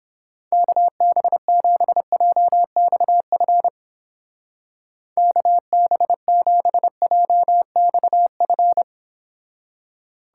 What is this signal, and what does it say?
Morse code, heard as a keyed tone, about 29 wpm, 700 Hz. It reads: KB7JXF KB7JXF